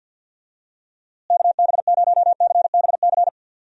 Morse code, heard as a keyed tone, about 51 wpm, 700 Hz. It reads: KB0CBC
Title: USA Call Signs - 6 Characters - Repeated 2x - 50wpm